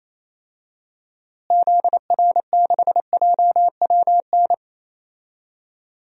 Morse code, heard as a keyed tone, about 28 wpm, 700 Hz. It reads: ZR6JWN
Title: DXCC-100 Call Signs - 6 Characters - Repeated 2x - 28wpm